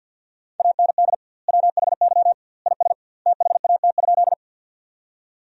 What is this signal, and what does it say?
Morse code, heard as a keyed tone, about 49 wpm, 700 Hz. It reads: AND WHY IS THAT?
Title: 2x Word-Spacing - Sentences from Top 200 Words - 50wpm